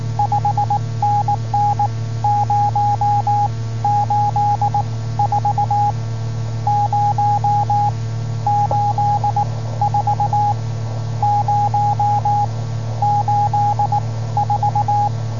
Morse code, heard as a keyed tone, about 19 wpm, spaced as for 13 wpm, 820 Hz, with radio noise. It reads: 5NN 084 084 084